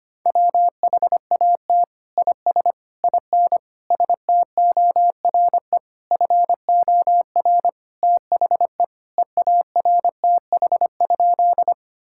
Morse code, heard as a keyed tone, about 25 wpm, 700 Hz. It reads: WHAT IS IN STORE FOR THE EARTH?
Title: Sentences from Top 1000 Words - Review Part 5 - 25wpm